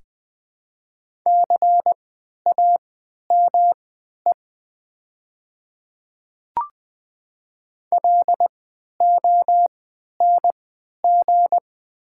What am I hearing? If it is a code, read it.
Morse code, heard as a keyed tone, about 20 wpm, spaced as for 11 wpm, 690 Hz. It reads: CAME LONG
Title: 3x Character Spacing - Top 300 Words - 20wpm